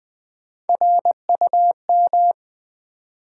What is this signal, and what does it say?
Morse code, heard as a keyed tone, about 20 wpm, 700 Hz. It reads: RUM
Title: Build A Word - 1 Letter At A Time - 5-Letter Words - 20wpm